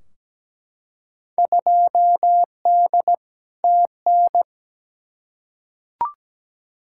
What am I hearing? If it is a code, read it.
Morse code, heard as a keyed tone, about 17 wpm, 700 Hz. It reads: 2D TN
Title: Field Day Exchanges Encore - 17wpm